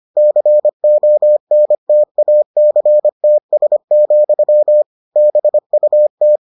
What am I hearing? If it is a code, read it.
Morse code, heard as a keyed tone, about 25 wpm, 600 Hz. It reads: CONTACTS, BUT